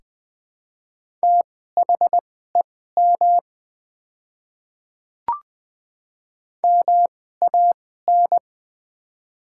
Morse code, heard as a keyed tone, about 20 wpm, spaced as for 14 wpm, 700 Hz. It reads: THEM MAN
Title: 2x Character Spacing - Top 200 Words - 20wpm